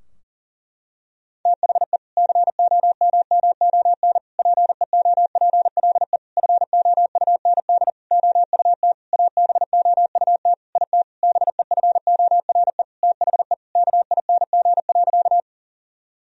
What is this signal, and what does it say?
Morse code, heard as a keyed tone, about 40 wpm, 700 Hz. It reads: THE COMMON PEOPLE FOUND OUT ABOUT IT BEFORE THE KING.
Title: Sentences from Top 1000 Words - Review Part 3 - 40wpm